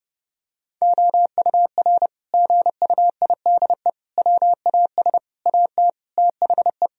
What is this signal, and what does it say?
Morse code, heard as a keyed tone, about 30 wpm, 700 Hz. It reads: OUR GUIDE WAS AT THE
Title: Sentences from Top 1000 Words - Review Part 3 - 30wpm